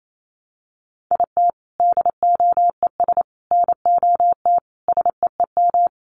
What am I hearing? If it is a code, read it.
Morse code, heard as a keyed tone, about 28 wpm, 700 Hz. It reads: IT DOES NOT SEEM
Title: Sets of 5 Words Encore2 28wpm